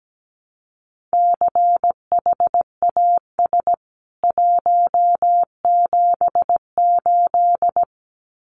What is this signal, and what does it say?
Morse code, heard as a keyed tone, about 17 wpm, 700 Hz. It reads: CHAS 178
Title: CWT Exchange 17wpm